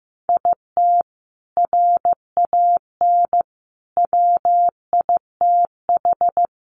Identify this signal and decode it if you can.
Morse code, heard as a keyed tone, about 15 wpm, 700 Hz. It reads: IT RAN WITH
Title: A Sound Of Thunder 15wpm